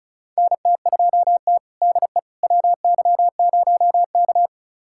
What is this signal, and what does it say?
Morse code, heard as a keyed tone, about 36 wpm, 700 Hz. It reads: NT2T DE WY0K
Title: USA Callsign DE Callsign - 4 Characters - 35wpm